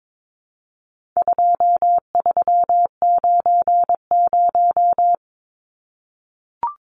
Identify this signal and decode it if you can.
Morse code, heard as a keyed tone, about 22 wpm, 700 Hz. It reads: 2390
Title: Numbers - 4-Digit 22wpm